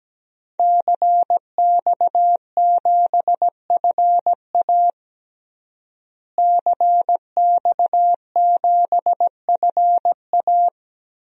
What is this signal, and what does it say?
Morse code, heard as a keyed tone, about 17 wpm, 700 Hz. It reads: CX7FA CX7FA
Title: DXCC-100 Call Signs - 5 Characters - Repeated 3x - 17wpm